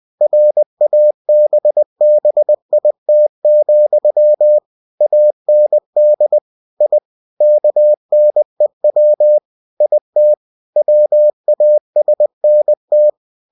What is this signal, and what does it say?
Morse code, heard as a keyed tone, about 20 wpm, 600 Hz. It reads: RABBIT, AND I KNEW IT WASNT
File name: cw-books / winnie-the-pooh / winnie-20wpm-page-0017